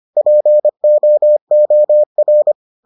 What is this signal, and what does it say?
Morse code, heard as a keyed tone, about 25 wpm, 600 Hz. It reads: POOR